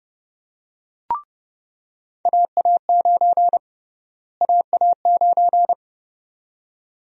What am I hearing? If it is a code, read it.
Morse code, heard as a keyed tone, about 30 wpm, 700 Hz. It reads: AA9 AA9
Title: USA Call Sign Prefixes - Repeated 2x - 30wpm